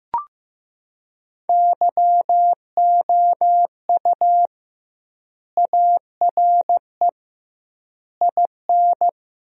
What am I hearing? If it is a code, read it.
Morse code, heard as a keyed tone, about 15 wpm, 700 Hz. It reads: YOU ARE IN
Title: 2x Word-Spacing - Sentences from Top 500 Words - 15wpm